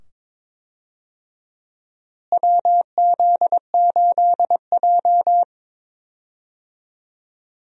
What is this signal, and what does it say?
Morse code, heard as a keyed tone, about 22 wpm, 700 Hz. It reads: WZ8J
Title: USA Call Signs - 4 Characters - Repeated 2x - 22wpm